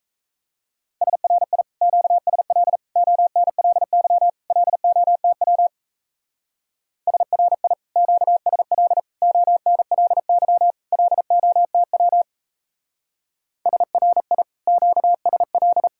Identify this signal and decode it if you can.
Morse code, heard as a keyed tone, about 37 wpm, 700 Hz. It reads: SRI QSL ONLY LOTW SRI QSL ONLY LOTW SRI QSL